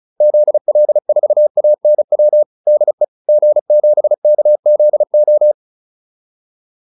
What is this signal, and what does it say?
Morse code, heard as a keyed tone, about 34 wpm, 600 Hz. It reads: ZL4ANW DE G7KZO
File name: QSO404 35